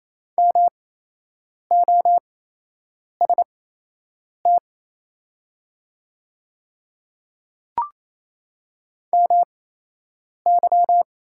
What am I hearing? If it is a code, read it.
Morse code, heard as a keyed tone, about 28 wpm, spaced as for 8 wpm, 690 Hz. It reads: MOST MY
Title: 8x Character Spacing - Top 100 Words - 28wpm